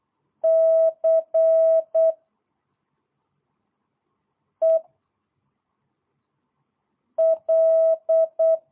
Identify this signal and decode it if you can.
Morse code, heard as a keyed tone, about 8 wpm, 660 Hz. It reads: C E L